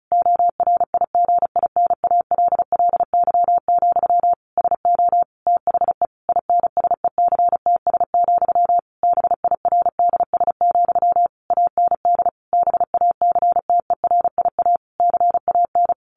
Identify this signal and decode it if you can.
Morse code, heard as a keyed tone, about 34 wpm, 700 Hz. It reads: ORIGINALLY, SO THE INSECTS, BIRDS, AND BACTERIA CAN
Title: A Sound Of Thunder 35wpm